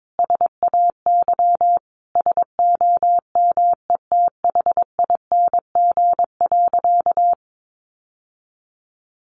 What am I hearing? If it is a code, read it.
Morse code, heard as a keyed tone, about 22 wpm, 700 Hz. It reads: SAY SOMETHING.
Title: Sentences from Top 600 Words 22wpm